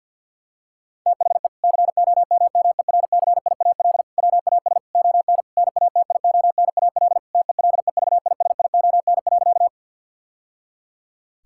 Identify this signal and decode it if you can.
Morse code, heard as a keyed tone, about 52 wpm, 700 Hz. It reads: THE COMMERCIAL WAS ON NATIONAL TELEVISION.
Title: Sentences from Top 1000 Words - Review Part 5 - 50wpm